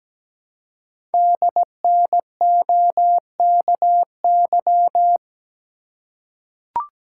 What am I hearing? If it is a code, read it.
Morse code, heard as a keyed tone, about 17 wpm, 700 Hz. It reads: DNOKY
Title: Code Groups - 5 Letters - 17wpm